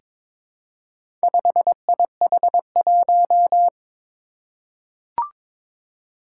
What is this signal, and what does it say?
Morse code, heard as a keyed tone, about 22 wpm, 700 Hz. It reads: 5IH1
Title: Four Challenging Letter-Number Combinations 22wpm